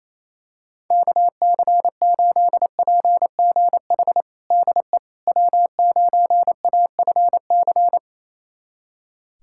Morse code, heard as a keyed tone, about 28 wpm, 700 Hz. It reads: KC8PGH DE W9AFC
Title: USA Callsign DE Callsign 28wpm